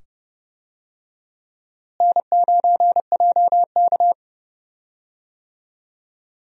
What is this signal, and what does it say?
Morse code, heard as a keyed tone, about 30 wpm, 700 Hz. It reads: N9JK
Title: USA Call Signs - 4 Characters - Repeated 2x - 30wpm